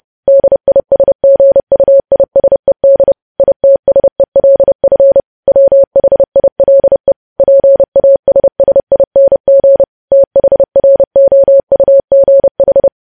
Morse code, heard as a keyed tone, about 30 wpm, 550 Hz. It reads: DISGUISED ITSELF WHILE PASSING THROUGH
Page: News Headlines in Morse Code at 30 WPM